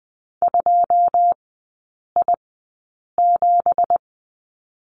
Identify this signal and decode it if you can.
Morse code, heard as a keyed tone, about 20 wpm, spaced as for 8 wpm, 700 Hz. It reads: 2I7
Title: 178 Lesson 29e - Slash - Three Characters 20wpm